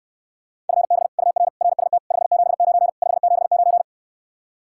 Morse code, heard as a keyed tone, about 69 wpm, 700 Hz. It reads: FB UR RST 579 579